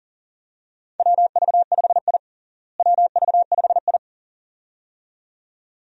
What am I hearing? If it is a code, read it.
Morse code, heard as a keyed tone, about 40 wpm, 700 Hz. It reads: WV5I WV5I
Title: USA Call Signs - 4 Characters - Repeated 2x - 40wpm